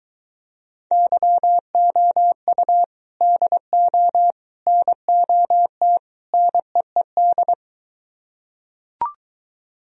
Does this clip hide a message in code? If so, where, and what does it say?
Yes: Morse code from 0.9 to 7.5 s, heard as a keyed tone, about 23 wpm, 700 Hz. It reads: YOU DO NOT NEED